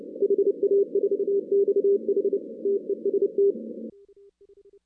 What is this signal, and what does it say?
Morse code, heard as a keyed tone, about 29 wpm, 410 Hz. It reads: HA4XH TEST
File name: 2016 arrl dx cw / 2016-ARRL-DX-CW-K5TR- - 0-46